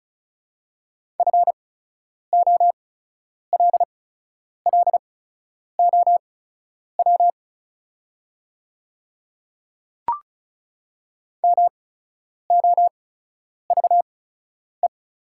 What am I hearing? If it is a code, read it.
Morse code, heard as a keyed tone, about 36 wpm, spaced as for 10 wpm, 690 Hz. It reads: FOLLOW MOVE
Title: 8x Character Spacing - Top 200 Words - 35wpm